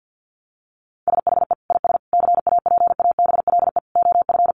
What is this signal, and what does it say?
Morse code, heard as a keyed tone, about 50 wpm, 700 Hz. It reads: SHE IS CAPABLE OF
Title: Sentences from Top 1200 Words 50wpm